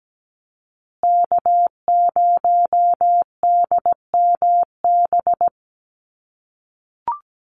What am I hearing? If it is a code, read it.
Morse code, heard as a keyed tone, about 17 wpm, 700 Hz. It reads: K0DMB
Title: USA Call Signs Repeated 2x 17wpm